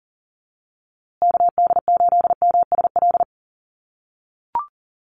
Morse code, heard as a keyed tone, about 40 wpm, 700 Hz. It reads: KD8MSL
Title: USA Call Signs Repeated 3x 40wpm